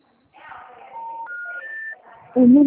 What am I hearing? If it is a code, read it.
Special information tone, with speech.